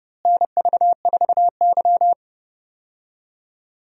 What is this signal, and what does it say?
Morse code, heard as a keyed tone, about 29 wpm, 700 Hz. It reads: NV4Y
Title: USA Call Signs - 4 Characters - Repeated 3x - 30wpm